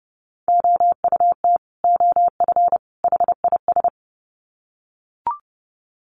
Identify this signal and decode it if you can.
Morse code, heard as a keyed tone, about 30 wpm, 700 Hz. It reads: OUT OF HIS
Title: Sets of 3 Words 30wpm